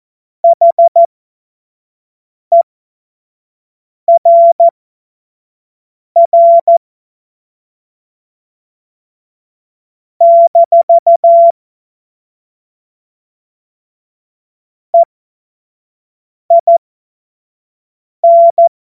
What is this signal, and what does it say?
Morse code, heard as a keyed tone, about 14 wpm, spaced as for 5 wpm, 680 Hz. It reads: HERR - EIN
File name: Morsekurs / Witze / Witz 20. Januar 2026 25 F70